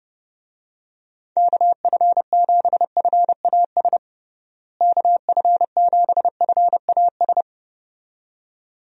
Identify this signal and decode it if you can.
Morse code, heard as a keyed tone, about 30 wpm, 700 Hz. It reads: KF7FAS KF7FAS
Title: USA Call Signs - 6 Characters - Repeated 2x - 30wpm